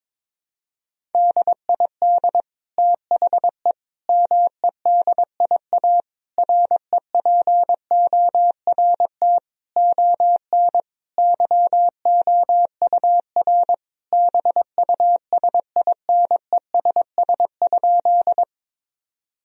Morse code, heard as a keyed tone, about 22 wpm, 700 Hz. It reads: DID THE MEDIA REPORT ON YOUR BUSINESS?
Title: Sentences from Top 1000 Words - Review Part 4 - 22wpm